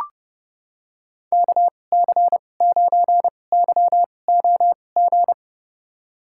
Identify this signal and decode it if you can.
Morse code, heard as a keyed tone, about 30 wpm, spaced as for 22 wpm, 700 Hz. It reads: KC9YOG
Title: USA Call Signs - 6 Characters - 2x Character Spacing - 30wpm